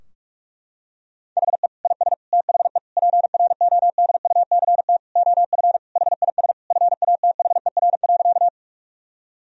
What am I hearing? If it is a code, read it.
Morse code, heard as a keyed tone, about 46 wpm, 700 Hz. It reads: HE IS THE PRODUCT OF HIS FATHER.
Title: Sentences from Top 1000 Words - Review Part 2 - 45wpm